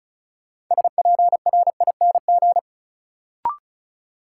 Morse code, heard as a keyed tone, about 36 wpm, 700 Hz. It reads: SPRING
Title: Sentences from Top 1200 Words - New Word Drill - 35wpm